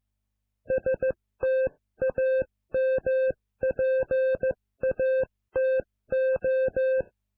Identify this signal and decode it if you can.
Morse code, heard as a keyed tone, about 15 wpm, spaced as for 13 wpm, 520 Hz. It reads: STAMPATO